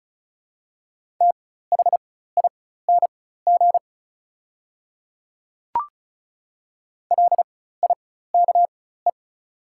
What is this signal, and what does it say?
Morse code, heard as a keyed tone, about 34 wpm, spaced as for 16 wpm, 700 Hz. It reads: THING LIKE
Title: 4x Character Spacing - Top 100 Words - 35wpm